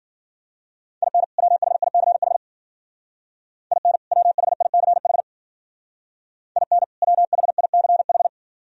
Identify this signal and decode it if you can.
Morse code, heard as a keyed tone, about 49 wpm, 700 Hz. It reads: IN WHICH IN WHICH IN WHICH
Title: Speed-Racing - Sets of 2 Words 40wpm